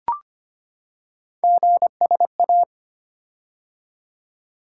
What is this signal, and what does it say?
Morse code, heard as a keyed tone, about 25 wpm, 700 Hz. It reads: GSA
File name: Three Letter-Number Combinations - 25wpm